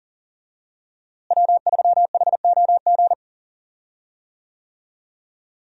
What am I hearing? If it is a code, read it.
Morse code, heard as a keyed tone, about 40 wpm, 700 Hz. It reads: W3HOG
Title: USA Call Signs - 5 Characters - Repeated 3x - 40wpm